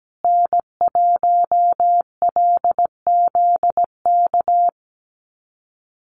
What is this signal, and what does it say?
Morse code, heard as a keyed tone, about 17 wpm, 700 Hz. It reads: N1LZK
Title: USA Call Signs Repeated 2x 17wpm